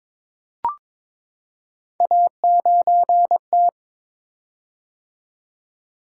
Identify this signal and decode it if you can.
Morse code, heard as a keyed tone, about 22 wpm, 700 Hz. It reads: A9T
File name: practice / 22wpm / Three Letter-Number Combinations - 22wpm